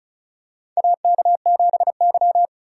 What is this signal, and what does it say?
Morse code, heard as a keyed tone, about 36 wpm, 700 Hz. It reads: AK7Y
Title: USA Call Signs - 4 Characters - 3x Character Spacing - 35wpm